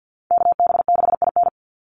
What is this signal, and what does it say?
Morse code, heard as a keyed tone, about 50 wpm, 700 Hz. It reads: KB6IN
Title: USA Call Signs 50wpm